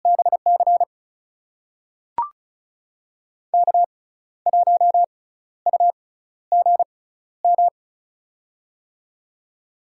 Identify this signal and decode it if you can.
Morse code, heard as a keyed tone, about 36 wpm, spaced as for 14 wpm, 700 Hz. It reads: BC K1UGM
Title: USA Call Signs - 6x Character Spacing - 35wpm